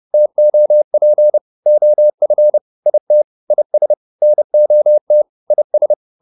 Morse code, heard as a keyed tone, about 30 wpm, 600 Hz. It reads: TOP OF IT IS NOT IS